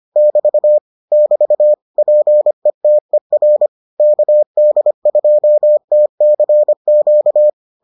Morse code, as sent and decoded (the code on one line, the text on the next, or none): -...- / -...- / .--. - . .-. / -.- -.. ..--- - -.-. --.-
<BT> <BT> PETER KD2TCQ